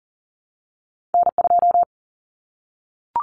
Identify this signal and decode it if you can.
Morse code, heard as a keyed tone, about 40 wpm, 700 Hz. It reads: N2
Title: Common Cut Numbers - 2-Digit 40wpm